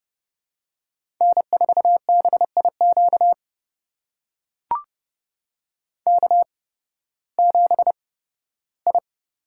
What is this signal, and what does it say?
Morse code, heard as a keyed tone, about 30 wpm, spaced as for 16 wpm, 700 Hz. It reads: N4BIQ K7I
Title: USA Call Signs - 5 Characters - 8x Character Spacing - 30wpm